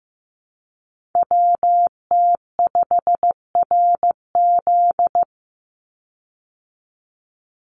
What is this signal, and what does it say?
Morse code, heard as a keyed tone, about 15 wpm, 700 Hz. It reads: WT5RZ
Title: Call Signs Repeated 3x - 15wpm